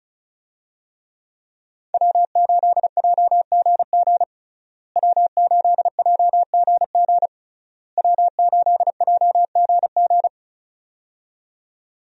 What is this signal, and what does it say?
Morse code, heard as a keyed tone, about 34 wpm, 700 Hz. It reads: W8JGG W8JGG W8JGG
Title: USA Call Signs - 5 Characters - Repeated 3x - 35wpm